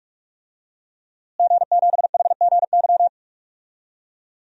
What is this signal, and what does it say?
Morse code, heard as a keyed tone, about 46 wpm, 700 Hz. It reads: G7HGY
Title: DXCC-100 Call Signs - 5 Characters - Repeated 2x - 45wpm